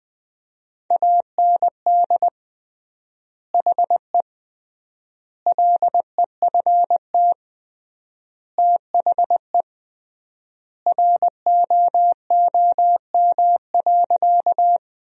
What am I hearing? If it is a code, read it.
Morse code, heard as a keyed tone, about 20 wpm, 700 Hz. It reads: AND HE LEFT THE ROOM.
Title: 3x Word-Spacing - Sentences from Top 300 Words - 20wpm